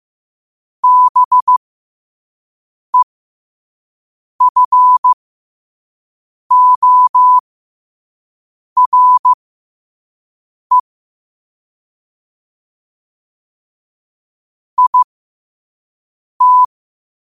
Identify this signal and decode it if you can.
Morse code, heard as a keyed tone, about 15 wpm, spaced as for 5 wpm, 1000 Hz. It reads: BEFORE IT